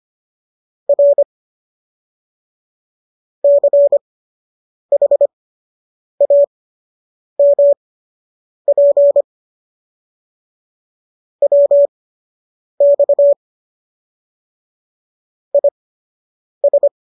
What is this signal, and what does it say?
Morse code, heard as a keyed tone, about 25 wpm, spaced as for 8 wpm, 570 Hz. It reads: R CHAMP WX IS